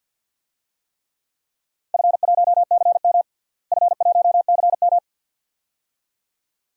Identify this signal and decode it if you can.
Morse code, heard as a keyed tone, about 52 wpm, 700 Hz. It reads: F1CM F1CM
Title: DXCC-100 Call Signs - 4 Characters - Repeated 2x - 50wpm